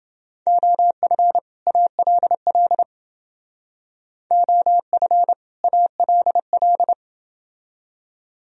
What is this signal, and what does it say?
Morse code, heard as a keyed tone, about 29 wpm, 700 Hz. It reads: OF ALL OF ALL